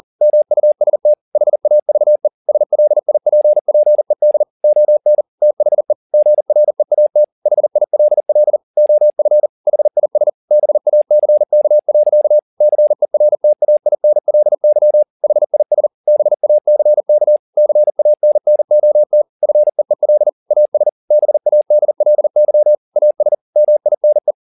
Morse code, heard as a keyed tone, about 40 wpm, 600 Hz. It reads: MUST HAVE SLIPPED ON THE GREAT HILL OF HIS BACK. CERTAINLY HIS BACK CANNOT FEEL AS BADLY AS MINE